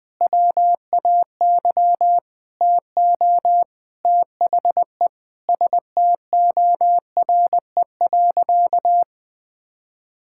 Morse code, heard as a keyed tone, about 21 wpm, 700 Hz. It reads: WAY TO THE STORE.
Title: Sentences with 8-Word Phrases - 20wpm